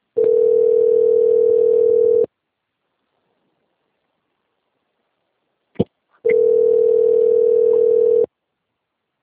Ringback tone (North America), 440 + 480 Hz.
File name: out- - - - - .8836